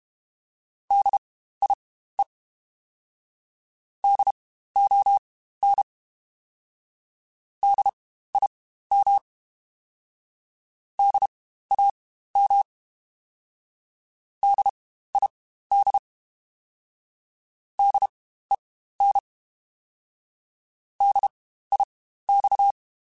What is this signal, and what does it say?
Morse code, heard as a keyed tone, about 32 wpm, spaced as for 15 wpm, 800 Hz. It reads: DIE DON DIM DAM DID DEN DIX
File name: Spaced Alphas / Spaced Alphas mp3s / Ds sp @30